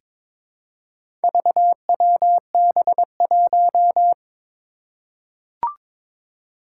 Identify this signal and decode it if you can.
Morse code, heard as a keyed tone, about 22 wpm, 700 Hz. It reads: VWB1
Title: Four Challenging Letter-Number Combinations 22wpm